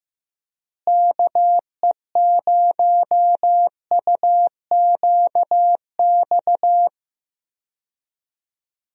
Morse code, heard as a keyed tone, about 15 wpm, 700 Hz. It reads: KE0UQX
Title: USA Call Signs - 6 Characters - Repeated 3x - 15wpm